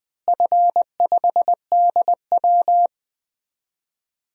Morse code, heard as a keyed tone, about 20 wpm, 700 Hz. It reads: F5DW
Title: DXCC-100 Call Signs - 4 Characters - Repeated 3x - 20wpm